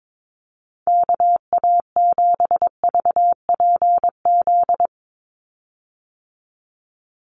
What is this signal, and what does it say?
Morse code, heard as a keyed tone, about 22 wpm, 700 Hz. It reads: KA7VPZ
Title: USA Call Signs 22wpm